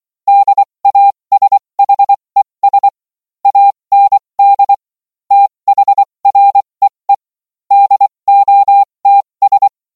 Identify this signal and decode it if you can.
Morse code, heard as a keyed tone, about 24 wpm, spaced as for 20 wpm, 800 Hz. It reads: DASHES AND THREE DOTS